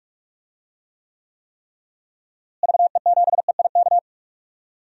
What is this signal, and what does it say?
Morse code, heard as a keyed tone, about 46 wpm, 700 Hz. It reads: VE7EIK
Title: DXCC-100 Call Signs - 6 Characters - Repeated 3x - 45wpm